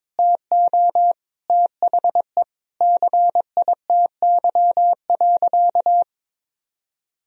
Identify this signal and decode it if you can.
Morse code, heard as a keyed tone, about 22 wpm, 700 Hz. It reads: TO THE CITY.
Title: Sentences from Top 1000 Words - Review Part 1 - 22wpm